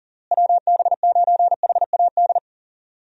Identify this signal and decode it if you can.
Morse code, heard as a keyed tone, about 40 wpm, 700 Hz. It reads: WB9HAD
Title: USA Call Signs - 6 Characters - Repeated 3x - 40wpm